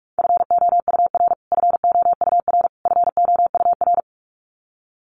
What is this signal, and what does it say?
Morse code, heard as a keyed tone, about 44 wpm, 700 Hz. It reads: FOUR FOUR FOUR
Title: Top 500 Words Repeated 45wpm